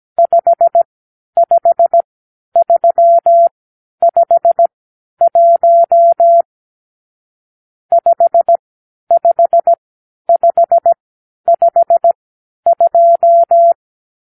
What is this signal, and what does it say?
Morse code, heard as a keyed tone, about 17 wpm, spaced as for 10 wpm, 680 Hz. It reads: 55351 55552
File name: arrl / Review (1-5)